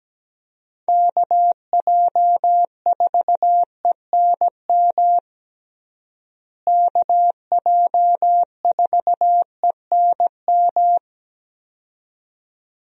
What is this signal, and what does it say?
Morse code, heard as a keyed tone, about 17 wpm, 700 Hz. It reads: KJ4ENM KJ4ENM
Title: USA Call Signs - 6 Characters - Repeated 2x - 17wpm